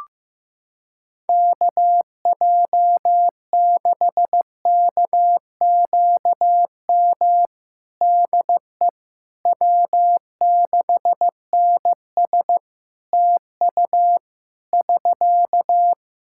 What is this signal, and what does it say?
Morse code, heard as a keyed tone, about 16 wpm, 700 Hz. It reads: KJ6KQM DE W6NS TU <SK>
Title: Practice QSO - Exchange 4 - Ending - 15wpm